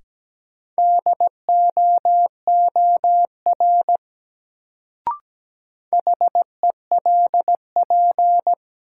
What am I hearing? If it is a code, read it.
Morse code, heard as a keyed tone, about 17 wpm, 700 Hz. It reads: DOOR HELP
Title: Sets of 2 Words - New Word Drill - 17wpm